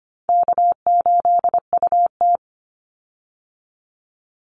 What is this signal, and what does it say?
Morse code, heard as a keyed tone, about 25 wpm, 700 Hz. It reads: K8UT
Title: Easy Call Signs Repeated 3x - 25wpm